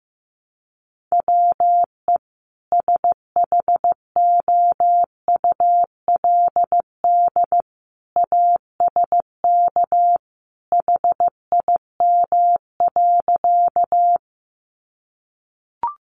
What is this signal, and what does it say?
Morse code, heard as a keyed tone, about 15 wpm, 700 Hz. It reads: WE SHOULD ASK HIM.
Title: Sentences from Top 300 Words 15wpm